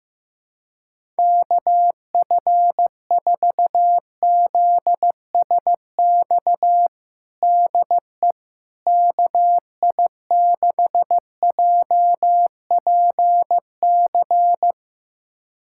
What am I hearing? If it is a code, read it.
Morse code, heard as a keyed tone, about 15 wpm, 700 Hz. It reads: KF4ZSX DE KI6JPC
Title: USA Callsign DE Callsign 15wpm